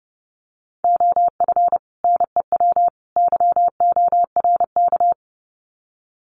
Morse code, heard as a keyed tone, about 30 wpm, 700 Hz. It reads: OF NEW YORK
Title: Sets of 3 Words 30wpm